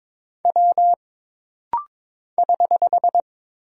Morse code, heard as a keyed tone, about 22 wpm, 700 Hz. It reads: W <HH>
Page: This rapid-fire timing is geared to push your ICR (Instant Character Recognition) skills.